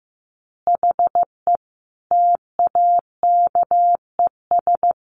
Morse code, heard as a keyed tone, about 15 wpm, 700 Hz. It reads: HE TAKES
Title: Additional Easy Phrases 15wpm